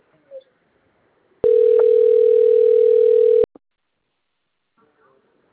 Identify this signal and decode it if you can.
Ringback tone (North America), 440 + 480 Hz.
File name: rg-1600-unknown- - - .77545